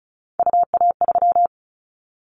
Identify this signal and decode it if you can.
Morse code, heard as a keyed tone, about 34 wpm, 700 Hz. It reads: UA3
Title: Callsign Prefixes for Top 25 DXCC - 35wpm